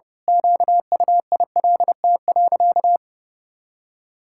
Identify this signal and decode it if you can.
Morse code, heard as a keyed tone, about 32 wpm, 700 Hz. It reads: QUILT.
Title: Pangram Challenge v2 - 30wpm